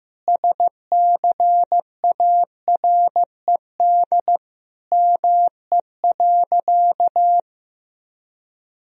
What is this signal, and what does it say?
Morse code, heard as a keyed tone, about 16 wpm, 700 Hz. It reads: SCARED ME.
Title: Sentences from Top 1000 Words - Review Part 4 - 15wpm